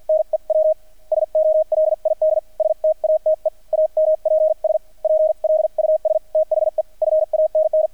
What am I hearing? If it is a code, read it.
Morse code, heard as a keyed tone, about 50 wpm, spaced as for 43 wpm, 630 Hz, with radio noise. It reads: NEW SOLID STATE AMPS PLUS THE FANN
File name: archived / CRSnet